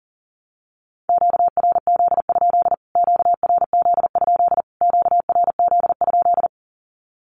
Morse code, heard as a keyed tone, about 40 wpm, 700 Hz. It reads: QRZ? QRZ? QRZ?
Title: Q-Codes Repeated 3x 40wpm